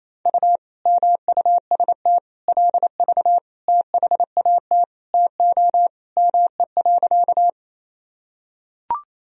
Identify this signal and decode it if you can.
Morse code, heard as a keyed tone, about 28 wpm, 700 Hz. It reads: U MUST LV THAT TO ME.
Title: Sentences from Top 400 Words and Abbreviations 28wpm